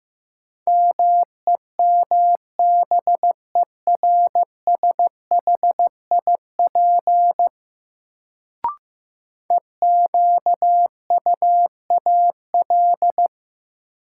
Morse code, heard as a keyed tone, about 15 wpm, 700 Hz. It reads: MEMBERSHIP EQUAL
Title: Sentences from Top 1400 Words - New Word Drill Alt - 15wpm